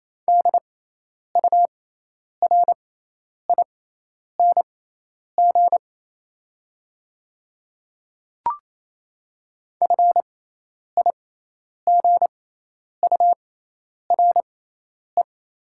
Morse code, heard as a keyed tone, about 28 wpm, spaced as for 10 wpm, 700 Hz. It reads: DURING FIGURE
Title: 6x Character Spacing - Top 500 Words - 28wpm